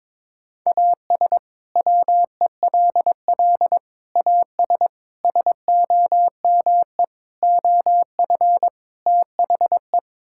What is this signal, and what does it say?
Morse code, heard as a keyed tone, about 22 wpm, 700 Hz. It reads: AS WELL AS SOME OF THE
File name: practice / 22wpm / Sets of 6 Words - 22wpm